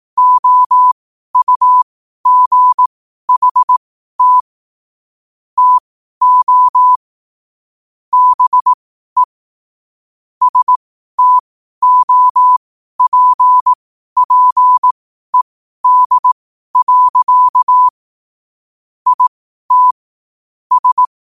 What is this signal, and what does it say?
Morse code, heard as a keyed tone, about 18 wpm, spaced as for 12 wpm, 1000 Hz. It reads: OUGHT TO BE STOPPED. IT S
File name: qotdfiles / 12wpm